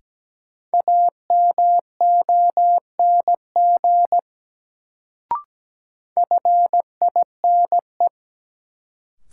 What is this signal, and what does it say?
Morse code, heard as a keyed tone, about 17 wpm, 700 Hz. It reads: AMONG FINE
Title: Sentences from Top 500 Words - New Word Drill - 17wpm